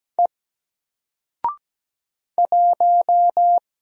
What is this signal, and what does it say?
Morse code, heard as a keyed tone, about 17 wpm, 700 Hz. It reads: E 1
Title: Single Letter-Number 17wpm